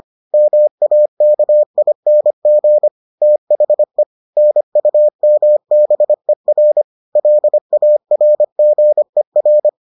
Morse code, heard as a keyed tone, about 25 wpm, 600 Hz. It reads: MAKING THE NUMBER LARGER